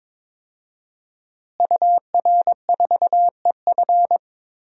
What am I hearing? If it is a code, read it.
Morse code, heard as a keyed tone, about 21 wpm, 700 Hz. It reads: UR4EF
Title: DXCC-100 Call Signs - 5 Characters - Repeated 2x - 22wpm